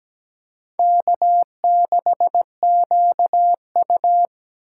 Morse code, heard as a keyed tone, about 17 wpm, 700 Hz. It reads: K6QU
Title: USA Call Signs - 2x Character Spacing - 17wpm